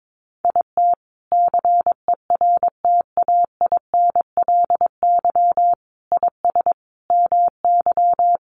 Morse code, heard as a keyed tone, about 22 wpm, 700 Hz. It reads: IT CERTAINLY IS MY
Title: Sentences from Top 1200 Words 22wpm